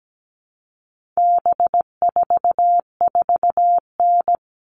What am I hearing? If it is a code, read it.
Morse code, heard as a keyed tone, about 17 wpm, 700 Hz. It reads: B44N
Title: All Cut Numbers - 4-Digit 17wpm